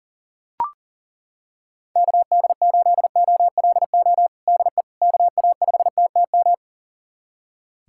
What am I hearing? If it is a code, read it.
Morse code, heard as a keyed tone, about 40 wpm, 700 Hz. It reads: KD8OLO DE KA5TTM
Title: USA Callsign DE Callsign 40wpm